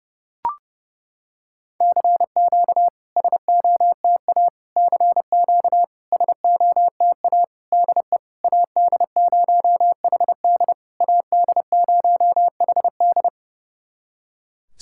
Morse code, heard as a keyed tone, about 30 wpm, 700 Hz. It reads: CQ SOTA CQ SOTA DE AD0HD AD0HD
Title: SOTA QSO Practice 30wpm